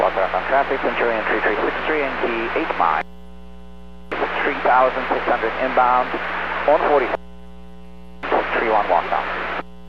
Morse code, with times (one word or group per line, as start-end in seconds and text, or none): none